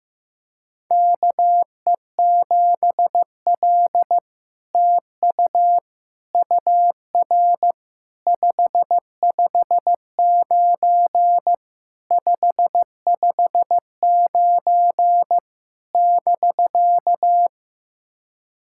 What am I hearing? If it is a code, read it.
Morse code, heard as a keyed tone, about 15 wpm, 700 Hz. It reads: KE7L TU UR 559 559 <BK>
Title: SOTA QSO Practice 15wpm